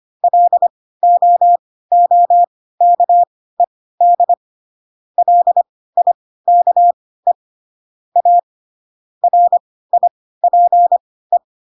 Morse code, heard as a keyed tone, about 25 wpm, spaced as for 16 wpm, 700 Hz. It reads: LOOKED LIKE A RIPE
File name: Pinocchio / Pinocchio ch1 10 25 ik1hge